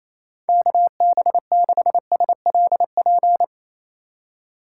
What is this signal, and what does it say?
Morse code, heard as a keyed tone, about 28 wpm, 700 Hz. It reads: KB6SLP
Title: USA Call Signs - 6 Characters - 2x Character Spacing - 28wpm